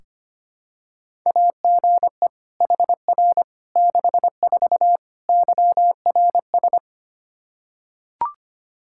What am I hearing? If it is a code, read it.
Morse code, heard as a keyed tone, about 25 wpm, 700 Hz. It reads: AGE HR 64 YRS
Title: K1USN SST QSO Practice 25wpm